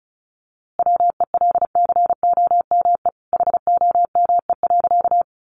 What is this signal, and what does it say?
Morse code, heard as a keyed tone, about 34 wpm, 700 Hz. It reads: WELCOME HOME.
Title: A Sound Of Thunder 35wpm